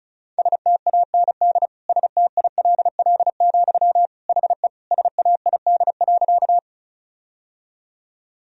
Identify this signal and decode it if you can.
Morse code, heard as a keyed tone, about 36 wpm, 700 Hz. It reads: STAND STILL, HE SAID.
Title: Sentences from Top 1000 Words - Review Part 1 - 35wpm